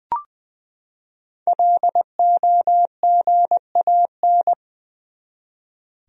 Morse code, heard as a keyed tone, about 20 wpm, 700 Hz. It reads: LOGAN
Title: USA Common Names - 20wpm